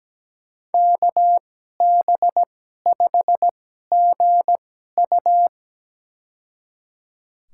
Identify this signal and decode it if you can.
Morse code, heard as a keyed tone, about 17 wpm, spaced as for 12 wpm, 700 Hz. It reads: KB5GU
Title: USA Call Signs - 5 Characters - 2x Character Spacing - 17wpm